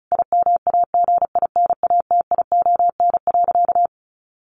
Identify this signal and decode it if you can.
Morse code, heard as a keyed tone, about 34 wpm, 700 Hz. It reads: IMAGINATION.
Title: Sentences from Top 1300 Words 35wpm